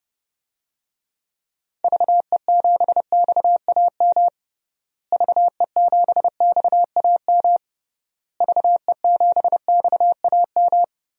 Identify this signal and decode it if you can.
Morse code, heard as a keyed tone, about 32 wpm, 700 Hz. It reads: VE7XAM VE7XAM VE7XAM
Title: DXCC-100 Call Signs - 6 Characters - Repeated 3x - 30wpm